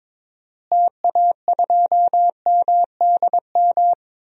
Morse code, heard as a keyed tone, about 22 wpm, 700 Hz. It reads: TA2MDM
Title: DXCC-100 Call Signs - 6 Characters - Repeated 2x - 22wpm